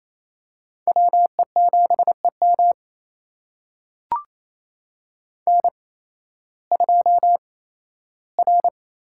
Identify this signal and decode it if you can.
Morse code, heard as a keyed tone, about 28 wpm, spaced as for 15 wpm, 700 Hz. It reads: WE7EM N2R
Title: USA Call Signs - 8x Character Spacing - 28wpm